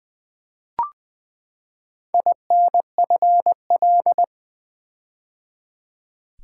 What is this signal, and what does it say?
Morse code, heard as a keyed tone, about 20 wpm, 700 Hz. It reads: INFL
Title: Build A Word - 4 Letters At A Time - 11-Letter Words - 20wpm